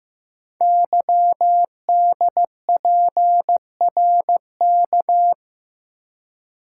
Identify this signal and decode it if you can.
Morse code, heard as a keyed tone, about 15 wpm, 700 Hz. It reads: YDPRK
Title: Code Groups - 5 Letters - 15wpm